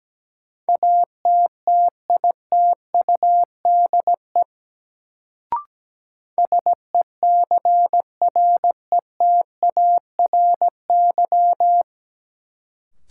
Morse code, heard as a keyed tone, about 17 wpm, 700 Hz. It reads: ATTITUDE SECRETARY
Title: Sentences from Top 1300 Words - New Word Drill - 17wpm